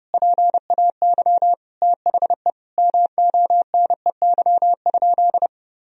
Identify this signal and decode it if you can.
Morse code, heard as a keyed tone, about 30 wpm, 700 Hz. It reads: PAY THE MONEY?
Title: Sentences from Top 1000 Words - Review Part 2 - 30wpm